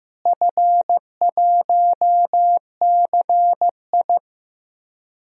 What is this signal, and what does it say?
Morse code, heard as a keyed tone, about 15 wpm, 700 Hz. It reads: F1CI
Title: DXCC-100 Call Signs - 4 Characters - Repeated 3x - 15wpm